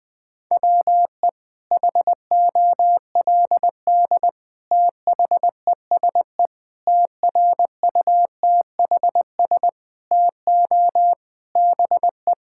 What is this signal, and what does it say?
Morse code, heard as a keyed tone, about 20 wpm, 700 Hz. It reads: WE HOLD THESE TRUTHS TO BE
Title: Sets of 9 Words - 20wpm